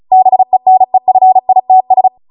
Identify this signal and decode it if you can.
Morse code, heard as a keyed tone, about 36 wpm, 750 Hz. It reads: BENEFITS